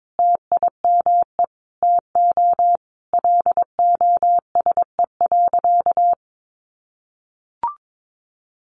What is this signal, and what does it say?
Morse code, heard as a keyed tone, about 22 wpm, 700 Hz. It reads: TIME TO LOSE.
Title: Sentences from Top 1100 Words 22wpm